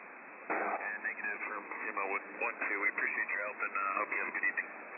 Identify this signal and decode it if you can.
Emergency Alert System (SAME) tones, heard as harsh buzzing bursts, with speech. It is heard from 1.2 to 4.6 s.